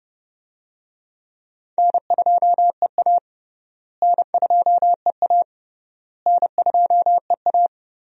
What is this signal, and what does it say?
Morse code, heard as a keyed tone, about 30 wpm, 700 Hz. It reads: N2EA N2EA N2EA
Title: USA Call Signs - 4 Characters - Repeated 3x - 30wpm